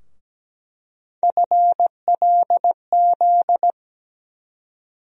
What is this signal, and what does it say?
Morse code, heard as a keyed tone, about 17 wpm, 700 Hz. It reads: FLZ
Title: USA Call Sign Postfixes - 3 Letters 17wpm